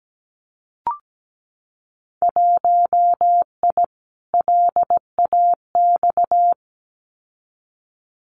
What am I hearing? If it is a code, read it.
Morse code, heard as a keyed tone, about 17 wpm, 700 Hz. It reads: 1I LAX
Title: Winter Field Day Exchanges Encore 17wpm